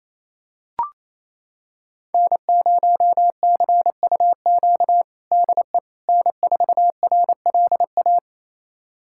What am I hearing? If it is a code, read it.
Morse code, heard as a keyed tone, about 27 wpm, 700 Hz. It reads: N0CUQ DE N4RLA
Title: USA Callsign DE Callsign - 5 Characters - 28wpm